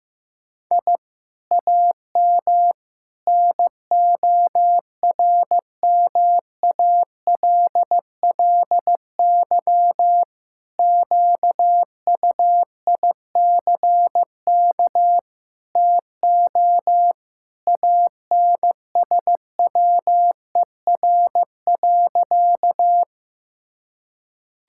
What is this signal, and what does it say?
Morse code, heard as a keyed tone, about 15 wpm, 700 Hz. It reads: I AM NORMALLY QUICK TO ANSWER.
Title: Sentences from Top 1000 Words - Review Part 6 - 15wpm